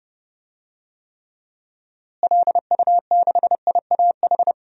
Morse code, heard as a keyed tone, about 30 wpm, 700 Hz. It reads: LU6IAH
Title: DXCC-100 Call Signs - 6 Characters - Repeated 2x - 30wpm